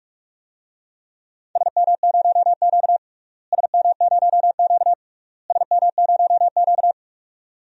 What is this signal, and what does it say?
Morse code, heard as a keyed tone, about 46 wpm, 700 Hz. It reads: SM0Q SM0Q SM0Q
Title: DXCC-100 Call Signs - 4 Characters - Repeated 3x - 45wpm